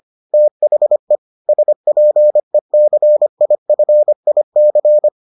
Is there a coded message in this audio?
Yes: Morse code from 0.3 to 5.1 s, heard as a keyed tone, about 25 wpm, 600 Hz. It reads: THE SPECIFIC